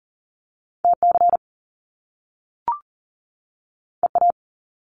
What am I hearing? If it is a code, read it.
Morse code, heard as a keyed tone, about 41 wpm, 700 Hz. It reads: TC EA
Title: Top Two Letter Combinations in Words 40wpm